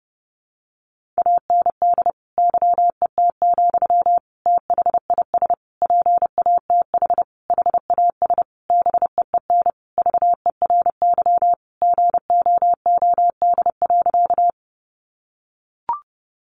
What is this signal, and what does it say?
Morse code, heard as a keyed tone, about 30 wpm, 700 Hz. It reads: AND YET, THIS PATH HAS BEEN VERY GOOD.
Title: Sentences from Top 1400 Words 30wpm